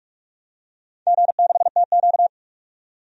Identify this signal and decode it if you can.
Morse code, heard as a keyed tone, about 42 wpm, 700 Hz. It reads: G6TQ
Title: DXCC-100 Call Signs - 4 Characters - 45wpm